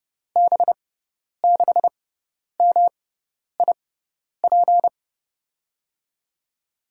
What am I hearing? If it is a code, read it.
Morse code, heard as a keyed tone, about 31 wpm, spaced as for 10 wpm, 700 Hz. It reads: B6MIP
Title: USA Call Signs - 6x Character Spacing - 30wpm